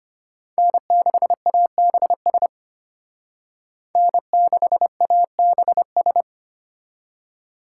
Morse code, heard as a keyed tone, about 31 wpm, 700 Hz. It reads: N6ABS N6ABS
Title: Speed Racing - USA 1x3 Call Signs 25wpm